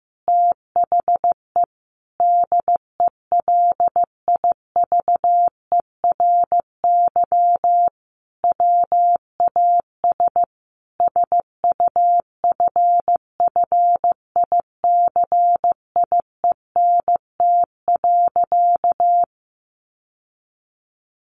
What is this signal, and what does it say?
Morse code, heard as a keyed tone, about 15 wpm, 700 Hz. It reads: THE DELIVERY WAS SUFFICIENT.
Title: Sentences from Top 1500 Words 15wpm